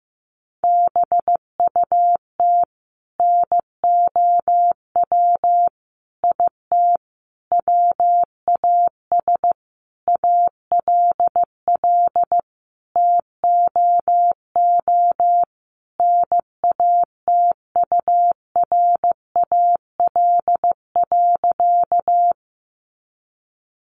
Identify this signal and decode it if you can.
Morse code, heard as a keyed tone, about 15 wpm, 700 Hz. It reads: BUT NOW IT WAS ALL TOO NATURAL.
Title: Sentences from Top 700 Words 15wpm